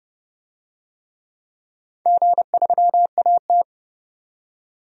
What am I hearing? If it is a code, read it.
Morse code, heard as a keyed tone, about 29 wpm, 700 Hz. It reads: G3AT
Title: DXCC-100 Call Signs - 4 Characters - 30wpm